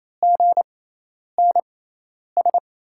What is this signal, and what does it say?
Morse code, heard as a keyed tone, about 27 wpm, spaced as for 10 wpm, 700 Hz. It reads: GNS
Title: USA Call Signs - 6x Character Spacing - 28wpm